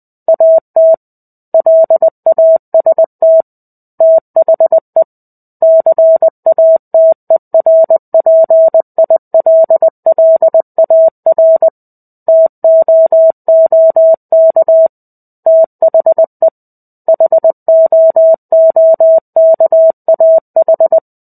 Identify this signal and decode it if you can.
Morse code, heard as a keyed tone, about 20 wpm, 650 Hz. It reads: AT LAST THE CATERPILLAR TOOK THE HOOKAH
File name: 05-Chapter-05 20 WPM